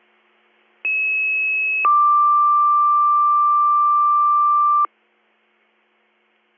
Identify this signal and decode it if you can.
Two-tone page, heard as a one-second tone, then a three-second tone, about 2600 then 1180 Hz. It is heard from 0.8 to 4.9 s.